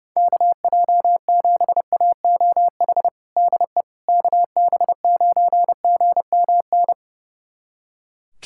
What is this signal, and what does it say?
Morse code, heard as a keyed tone, about 30 wpm, 700 Hz. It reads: KJ7AOH DE KB9GMN
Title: USA Callsign DE Callsign - 6 Characters - 30wpm